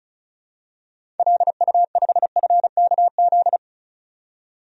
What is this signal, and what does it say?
Morse code, heard as a keyed tone, about 36 wpm, 700 Hz. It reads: LU5FKZ
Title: DXCC-100 Call Signs - 6 Characters - 35wpm